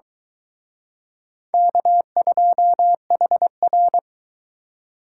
Morse code, heard as a keyed tone, about 23 wpm, 700 Hz. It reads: K2HR